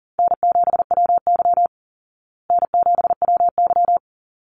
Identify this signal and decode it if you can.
Morse code, heard as a keyed tone, about 40 wpm, 700 Hz. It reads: N7WY N7WY
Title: Easy Call Signs Repeated 2x - 40wpm